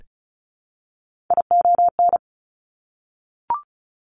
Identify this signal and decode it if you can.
Morse code, heard as a keyed tone, about 36 wpm, 700 Hz. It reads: ION
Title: Top Three Letter Combinations in Words 35wpm